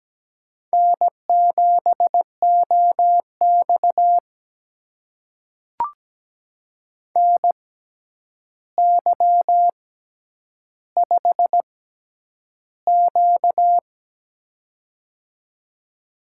Morse code, heard as a keyed tone, about 17 wpm, spaced as for 9 wpm, 700 Hz. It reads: N7OX NY5Q
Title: USA Call Signs - 4 Characters - 6x Character Spacing - 17wpm